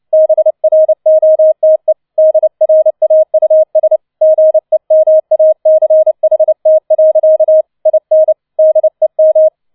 Morse code, heard as a keyed tone, about 29 wpm, 630 Hz. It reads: BROTE DRAUS GEMACHT. IN DEM
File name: 006 Sechster Streich 30wpm